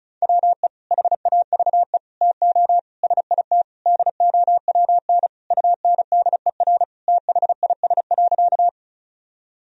Morse code, heard as a keyed tone, about 36 wpm, 700 Hz. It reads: WE HAVE TO SIT DOWN UNDER THIS.
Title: Sentences from Top 1000 Words - Review Part 1 - 35wpm